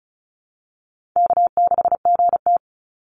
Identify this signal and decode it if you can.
Morse code, heard as a keyed tone, about 34 wpm, 700 Hz. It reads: K6GT
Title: Call Signs - 35wpm